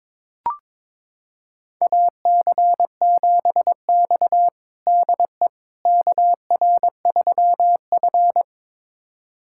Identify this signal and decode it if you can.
Morse code, heard as a keyed tone, about 22 wpm, 700 Hz. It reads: AC7X DE KR3F
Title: USA Callsign DE Callsign - 4 Characters - 22wpm